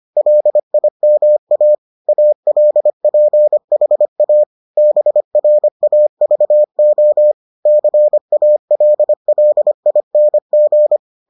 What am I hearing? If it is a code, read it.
Morse code, heard as a keyed tone, about 25 wpm, 600 Hz. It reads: LIMA ALPHA BRAVO CALLING